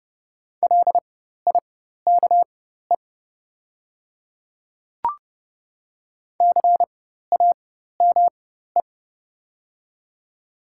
Morse code, heard as a keyed tone, about 30 wpm, spaced as for 14 wpm, 700 Hz. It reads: LIKE CAME
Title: 4x Character Spacing - Top 200 Words - 30wpm